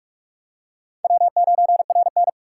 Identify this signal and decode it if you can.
Morse code, heard as a keyed tone, about 37 wpm, 700 Hz. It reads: W9RN
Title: Speed Racing - USA 1x2 Call Signs 30wpm